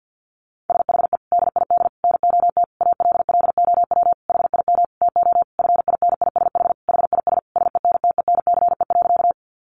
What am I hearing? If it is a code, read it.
Morse code, heard as a keyed tone, about 50 wpm, 700 Hz. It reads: SHE DID NOT ALLOW HIM TO FINISH HIS SENTENCE.
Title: Sentences from Top 1200 Words 50wpm